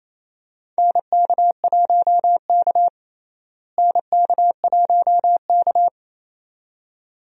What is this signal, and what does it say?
Morse code, heard as a keyed tone, about 28 wpm, 700 Hz. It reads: NK1K NK1K
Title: USA Call Signs - 4 Characters - Repeated 2x - 28wpm